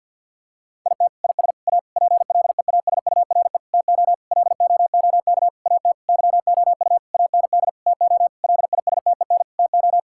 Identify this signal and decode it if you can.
Morse code, heard as a keyed tone, about 50 wpm, 700 Hz. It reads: IT IS A PLEASURE TO LOOK AT YOU AND TO LISTEN TO
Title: Sentences from Top 1600 Words 50wpm